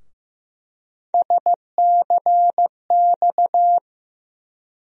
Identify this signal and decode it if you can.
Morse code, heard as a keyed tone, about 15 wpm, 700 Hz. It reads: SCX
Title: USA Call Sign Postfixes - 3 Letters 15wpm